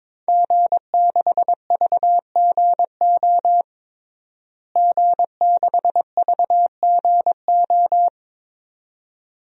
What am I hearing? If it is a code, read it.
Morse code, heard as a keyed tone, about 22 wpm, 700 Hz. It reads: G6VGO G6VGO
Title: DXCC-100 Call Signs - 5 Characters - Repeated 2x - 22wpm